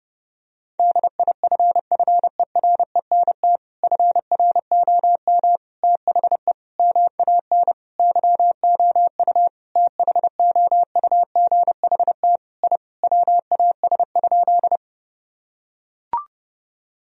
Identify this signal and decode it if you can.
Morse code, heard as a keyed tone, about 32 wpm, 700 Hz. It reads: DIFFERENT FROM THE MAN YOU THOUGHT I WAS?
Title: Sentences from Top 1000 Words - Review Part 3 - 30wpm